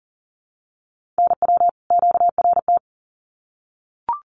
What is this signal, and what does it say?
Morse code, heard as a keyed tone, about 40 wpm, 700 Hz. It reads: NW QRT
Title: Common QSO Phrases 40wpm